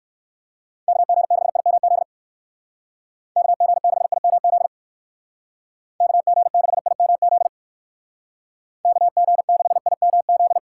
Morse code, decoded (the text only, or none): KG6IMZ KG6IMZ KG6IMZ KG6IMZ